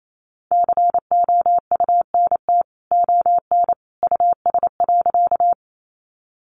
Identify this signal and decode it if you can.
Morse code, heard as a keyed tone, about 28 wpm, 700 Hz. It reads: COUNT ON US.
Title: Sentences from Top 1500 Words 28wpm